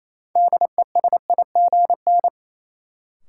Morse code, heard as a keyed tone, about 28 wpm, 700 Hz. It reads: DESIGN
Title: Sentences from Top 800 Words - New Word Drill - 28wpm